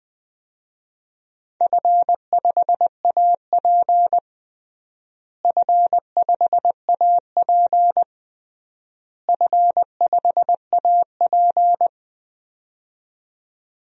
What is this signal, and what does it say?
Morse code, heard as a keyed tone, about 20 wpm, 700 Hz. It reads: F5AP F5AP F5AP
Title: DXCC-100 Call Signs - 4 Characters - Repeated 3x - 20wpm